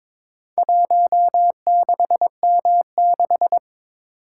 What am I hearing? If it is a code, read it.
Morse code, heard as a keyed tone, about 22 wpm, 700 Hz. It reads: 16M6
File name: practice / 22wpm / Four Letter-Number Combinations - 22wpm